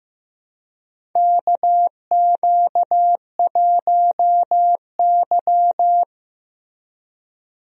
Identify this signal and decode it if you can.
Morse code, heard as a keyed tone, about 15 wpm, 700 Hz. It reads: KQ1Y
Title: USA Call Signs - 4 Characters - Repeated 2x - 15wpm